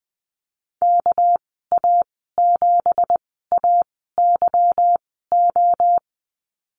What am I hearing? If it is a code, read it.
Morse code, heard as a keyed tone, about 20 wpm, spaced as for 14 wpm, 700 Hz. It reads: KA7AYO
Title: 219 Lesson 35g - Number 0 - Call Signs 20wpm